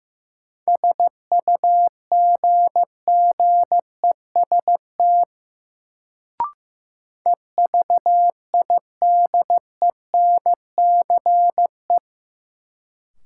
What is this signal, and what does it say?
Morse code, heard as a keyed tone, about 15 wpm, 700 Hz. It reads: SUGGEST EVIDENCE
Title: Sentences from Top 1300 Words - New Word Drill Alt - 15wpm